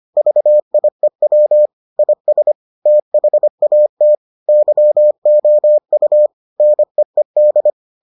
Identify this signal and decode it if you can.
Morse code, heard as a keyed tone, about 25 wpm, 600 Hz. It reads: VIEW IS THAT YOU NEED